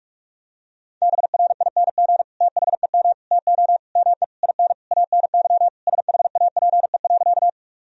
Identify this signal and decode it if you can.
Morse code, heard as a keyed tone, about 47 wpm, 700 Hz. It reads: BRING THEM TO ME IN ANY SHAPE.
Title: Sentences from Top 1000 Words - Review Part 3 - 45wpm